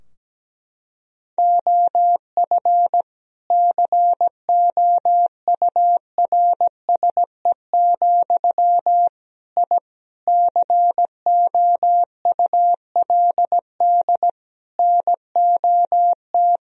Morse code, heard as a keyed tone, about 17 wpm, 700 Hz. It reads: OF COURSE, I COULD NOT
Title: Sentences from Top 1000 Words - Review Part 3 - 17wpm